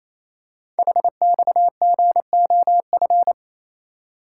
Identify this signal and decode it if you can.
Morse code, heard as a keyed tone, about 27 wpm, 700 Hz. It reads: HXGOF
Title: Code Groups - 5 Letters - 28wpm